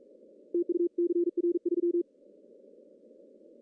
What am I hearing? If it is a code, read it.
Morse code, heard as a keyed tone, about 44 wpm, 380 Hz. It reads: TU CR3
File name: CW / cqww-cw-k5tr